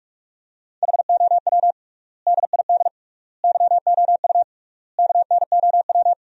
Morse code, heard as a keyed tone, about 46 wpm, 700 Hz. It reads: HOW DID YOU KNOW
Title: 3x Word-Spacing - Sentences from Top 100 Words - 45wpm